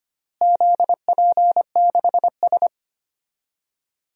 Morse code, heard as a keyed tone, about 25 wpm, 700 Hz. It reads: ZP6S
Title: DXCC-100 Call Signs - 4 Characters - Repeated 3x - 25wpm